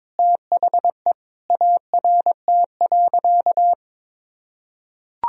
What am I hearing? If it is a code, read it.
Morse code, heard as a keyed tone, about 21 wpm, 700 Hz. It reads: THE ART.
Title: Sentences from Top 1000 Words - Review Part 3 - 22wpm